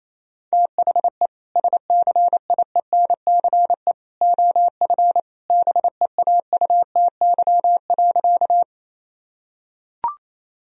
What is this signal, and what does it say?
Morse code, heard as a keyed tone, about 28 wpm, 700 Hz. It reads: THE SCIENCE OF BEAUTY.
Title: Sentences from Top 1000 Words - Review Part 2 - 28wpm